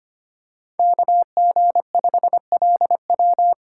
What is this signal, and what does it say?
Morse code, heard as a keyed tone, about 25 wpm, 700 Hz. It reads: KG5LW
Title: USA Call Signs - 5 Characters - 4x Character Spacing - 25wpm